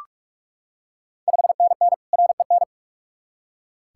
Morse code, heard as a keyed tone, about 44 wpm, 690 Hz. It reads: 5NN REN
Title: Ontario QSO Party Exchange 45wpm